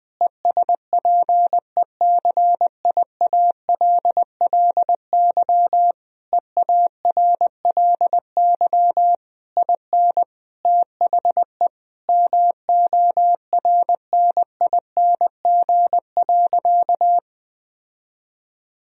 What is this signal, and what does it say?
Morse code, heard as a keyed tone, about 20 wpm, 700 Hz. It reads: ESPECIALLY EARLY IN THE MORNING.
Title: Sentences from Top 1000 Words - Review Part 4 - 20wpm